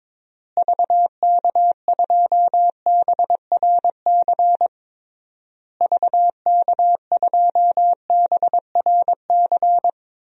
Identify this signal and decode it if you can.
Morse code, heard as a keyed tone, about 22 wpm, 700 Hz. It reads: VK2BRC VK2BRC
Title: DXCC-100 Call Signs - 6 Characters - Repeated 3x - 22wpm